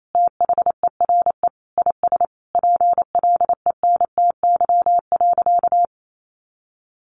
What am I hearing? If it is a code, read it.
Morse code, heard as a keyed tone, about 28 wpm, 700 Hz. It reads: THERE IS PLENTY.
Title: Sentences from Top 1300 Words 28wpm